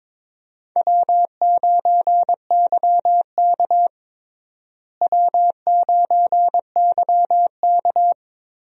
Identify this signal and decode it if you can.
Morse code, heard as a keyed tone, about 22 wpm, 700 Hz. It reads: W9YK W9YK
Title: USA Call Signs - 4 Characters - Repeated 3x - 22wpm